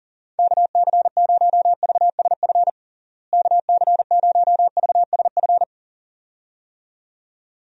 Morse code, heard as a keyed tone, about 40 wpm, 700 Hz. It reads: KC0VSF KC0VSF
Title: USA Call Signs - 6 Characters - Repeated 2x - 40wpm